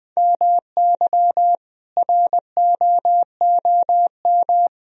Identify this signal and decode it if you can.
Morse code, heard as a keyed tone, about 20 wpm, 700 Hz. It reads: MY ROOM
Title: Sentences from Top 1000 Words - Review Part 1 - 20wpm